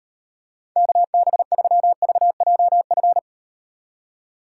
Morse code, heard as a keyed tone, about 38 wpm, 700 Hz. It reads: KB3VJF